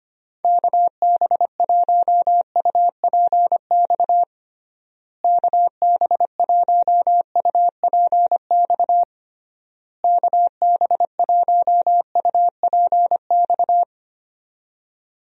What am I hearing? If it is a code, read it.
Morse code, heard as a keyed tone, about 25 wpm, 700 Hz. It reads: KB1UPX KB1UPX KB1UPX
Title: USA Call Signs - 6 Characters - Repeated 3x - 25wpm